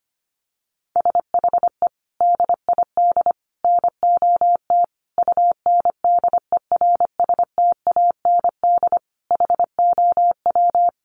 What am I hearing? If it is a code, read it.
Morse code, heard as a keyed tone, about 25 wpm, 700 Hz. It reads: SHE DID NOT UNDERSTAND HOW
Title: Sentences from Top 600 Words 25wpm